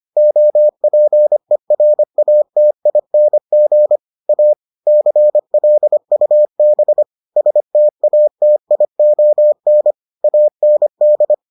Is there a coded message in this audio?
Yes: Morse code from 0.2 to 11.3 s, heard as a keyed tone, about 25 wpm, 600 Hz. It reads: OPERATING A CLUB STATION AND